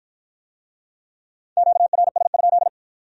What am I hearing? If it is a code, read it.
Morse code, heard as a keyed tone, about 53 wpm, 700 Hz. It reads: QRS?